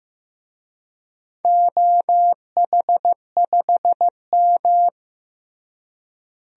Morse code, heard as a keyed tone, about 15 wpm, 700 Hz. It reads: OH5M
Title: DXCC-100 Call Signs - 4 Characters - Repeated 3x - 15wpm